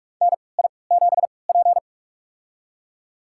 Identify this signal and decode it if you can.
Morse code, heard as a keyed tone, about 41 wpm, spaced as for 25 wpm, 700 Hz. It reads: NI7P
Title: USA Call Signs - 4 Characters - 3x Character Spacing - 45wpm